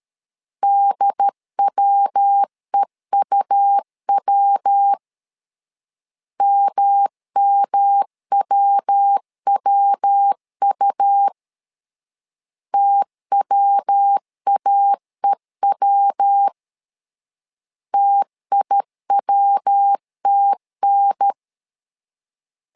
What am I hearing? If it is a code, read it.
Morse code, heard as a keyed tone, about 13 wpm, 800 Hz. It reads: DWEUW MMWWU TWAEW TIWTN
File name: LEZIONI / 3 MWRO / 3 MW 12wpm 1633